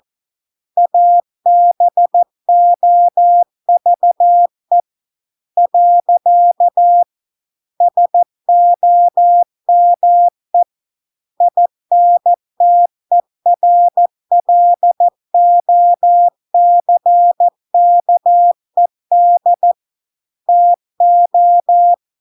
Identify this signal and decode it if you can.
Morse code, heard as a keyed tone, about 14 wpm, 700 Hz. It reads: ABOVE . SOME INTERLOCKED TO